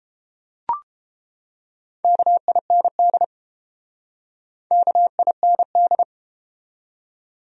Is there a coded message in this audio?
Yes: Morse code from 2.0 to 6.0 s, heard as a keyed tone, about 30 wpm, 700 Hz. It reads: KIND KIND